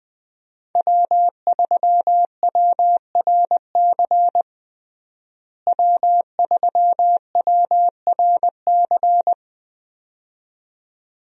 Morse code, heard as a keyed tone, about 20 wpm, 700 Hz. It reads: W3WRC W3WRC
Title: USA Call Signs - 5 Characters - Repeated 2x - 20wpm